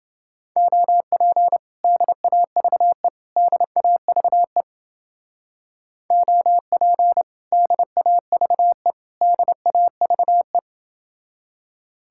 Morse code, heard as a keyed tone, about 30 wpm, 700 Hz. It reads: OP DAVE DAVE OP DAVE DAVE